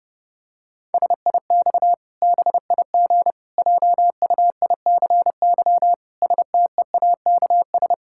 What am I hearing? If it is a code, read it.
Morse code, heard as a keyed tone, about 32 wpm, 700 Hz. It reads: SIX BIG JUICY STEAKS
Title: Pangram Challenge v2 - 30wpm